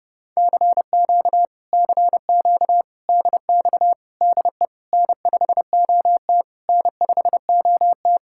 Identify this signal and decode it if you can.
Morse code, heard as a keyed tone, about 29 wpm, 700 Hz. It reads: CQ CQ DX DE N5OT N5OT